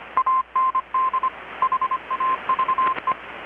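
Morse code, heard as a keyed tone, about 25 wpm, 1040 Hz. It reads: AND HAVE
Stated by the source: Morse from KPH's Night of Nights XX maritime radiotelegraphy event, received in Chiapas, Mexico about 3,800 km away on a Kenwood R-600 with a grounded 10 m long-wire antenna.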